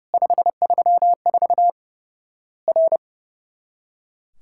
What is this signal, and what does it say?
Morse code, heard as a keyed tone, about 30 wpm, 700 Hz. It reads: 534 R
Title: Worked All Europe DX Contest - QTC Exchange - 30wpm